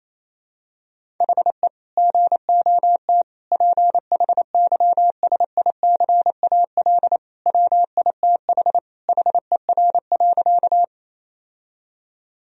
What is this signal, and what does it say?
Morse code, heard as a keyed tone, about 28 wpm, 700 Hz. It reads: HE GOT PHYSICAL WITH HER.
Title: Sentences from Top 1000 Words - Review Part 4 - 28wpm